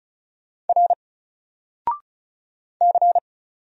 Morse code, heard as a keyed tone, about 35 wpm, 700 Hz. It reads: R C
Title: Single Letter-Number Alt 35wpm